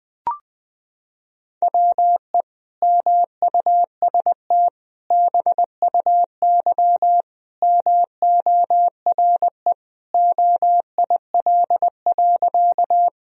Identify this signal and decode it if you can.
Morse code, heard as a keyed tone, about 21 wpm, 700 Hz. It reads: WE MUST BUY MORE OIL.
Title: Sentences from Top 1000 Words - Review Part 4 - 20wpm